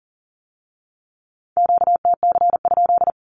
Morse code, heard as a keyed tone, about 41 wpm, 700 Hz. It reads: QTC?
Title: Q-Codes with Spoken Meaning - 40wpm